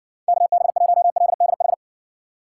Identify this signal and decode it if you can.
Morse code, heard as a keyed tone, about 60 wpm, 700 Hz. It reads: KB1LDH